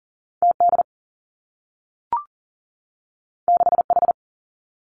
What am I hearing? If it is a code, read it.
Morse code, heard as a keyed tone, about 40 wpm, 700 Hz. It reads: TD 6H
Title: Two Challenging Letter-Number Combinations 40wpm